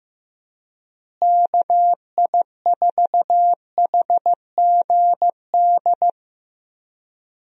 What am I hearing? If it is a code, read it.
Morse code, heard as a keyed tone, about 15 wpm, 700 Hz. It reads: KI4HGD
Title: USA Call Signs - 6 Characters - Repeated 2x - 15wpm